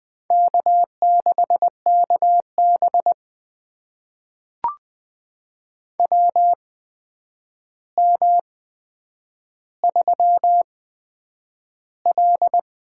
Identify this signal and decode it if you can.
Morse code, heard as a keyed tone, about 20 wpm, spaced as for 9 wpm, 700 Hz. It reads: K6KB WM3L
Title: USA Call Signs - 4 Characters - 8x Character Spacing - 20wpm